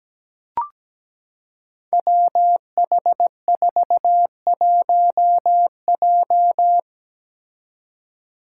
Morse code, heard as a keyed tone, about 17 wpm, 700 Hz. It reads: WH41J
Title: Five Challenging Letter-Number Combinations 17wpm